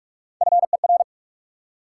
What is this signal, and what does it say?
Morse code, heard as a keyed tone, about 44 wpm, 700 Hz. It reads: FER
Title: Top 90 QSO Elements 45wpm